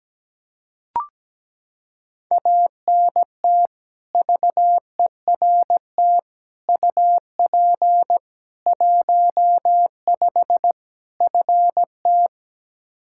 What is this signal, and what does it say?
Morse code, heard as a keyed tone, about 17 wpm, 700 Hz. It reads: ANT VERT UP 15 FT
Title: Practice QSO - Exchange 2 - Info - 17wpm